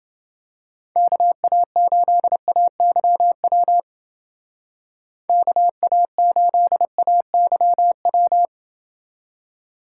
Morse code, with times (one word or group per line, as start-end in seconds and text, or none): 1.0-3.8 KA8AYW
5.3-8.5 KA8AYW